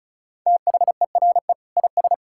Morse code, heard as a keyed tone, about 36 wpm, 700 Hz. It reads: THERE IS
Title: Sentences from Top 1000 Words - Review Part 6 - 35wpm